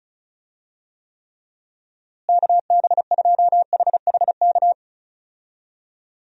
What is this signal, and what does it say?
Morse code, heard as a keyed tone, about 34 wpm, 700 Hz. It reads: KB2HHK
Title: USA Call Signs - 6 Characters - Repeated 2x - 35wpm